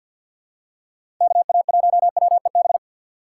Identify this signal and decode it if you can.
Morse code, heard as a keyed tone, about 46 wpm, 700 Hz. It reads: KA1WEB
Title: USA Call Signs - 3x Character Spacing - 50wpm